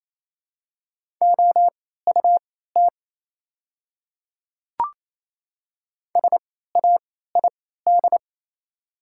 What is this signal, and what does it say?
Morse code, heard as a keyed tone, about 28 wpm, spaced as for 16 wpm, 690 Hz. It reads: OUT SAID
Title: 3x Character Spacing - Top 100 Words - 28wpm